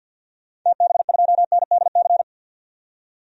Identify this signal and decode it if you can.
Morse code, heard as a keyed tone, about 52 wpm, 700 Hz. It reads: TB2NDC
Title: DXCC-100 Call Signs - 6 Characters - 50wpm